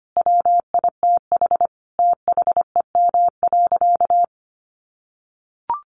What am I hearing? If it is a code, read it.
Morse code, heard as a keyed tone, about 25 wpm, 700 Hz. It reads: WITH THEM.
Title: Sentences from Top 700 Words 25wpm